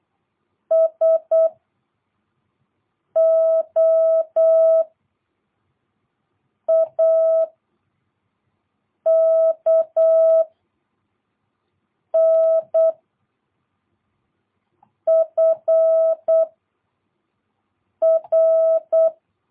Morse code, heard as a keyed tone, about 8 wpm, 660 Hz. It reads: S O A K N F R